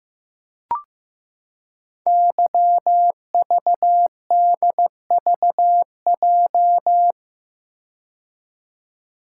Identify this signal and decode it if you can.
Morse code, heard as a keyed tone, about 15 wpm, 700 Hz. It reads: YVDVJ
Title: Code Groups - 5 Letters - 15wpm